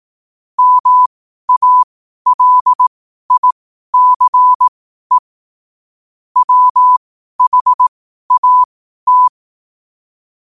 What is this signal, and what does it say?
Morse code, heard as a keyed tone, about 18 wpm, spaced as for 12 wpm, 1000 Hz. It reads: MALICE WHAT
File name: qotdfiles / 12wpm